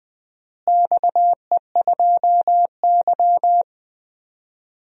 Morse code, heard as a keyed tone, about 20 wpm, 700 Hz. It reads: XE2Y
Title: DXCC-100 Call Signs - 4 Characters - Repeated 2x - 20wpm